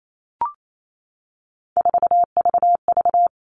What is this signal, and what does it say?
Morse code, heard as a keyed tone, about 28 wpm, 700 Hz. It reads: 4VV
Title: All Cut Numbers - 3-Digit 28wpm